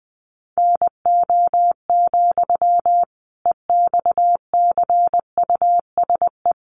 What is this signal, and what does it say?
Morse code, heard as a keyed tone, about 20 wpm, 700 Hz. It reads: NO, EXCUSE
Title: Sentences from Top 700 Words 20wpm